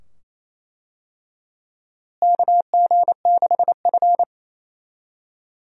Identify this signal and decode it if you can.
Morse code, heard as a keyed tone, about 27 wpm, 700 Hz. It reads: KG6F
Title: USA Call Signs - 4 Characters - Repeated 2x - 28wpm